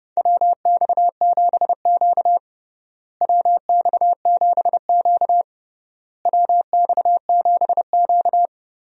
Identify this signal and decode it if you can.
Morse code, heard as a keyed tone, about 30 wpm, 700 Hz. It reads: WX7Q WX7Q WX7Q
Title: USA Call Signs - 4 Characters - Repeated 3x - 30wpm